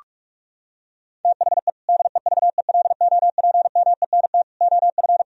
Morse code, heard as a keyed tone, about 41 wpm, 700 Hz. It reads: THE DEVELOPMENT OF
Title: Sentences from Top 1000 Words - Review Part 4 - 45wpm